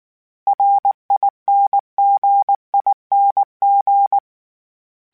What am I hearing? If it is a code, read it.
Morse code, heard as a keyed tone, about 19 wpm, 800 Hz. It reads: RINGING
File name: Special INGs-mp3s / Special INGs-mp3s / Dble ings @18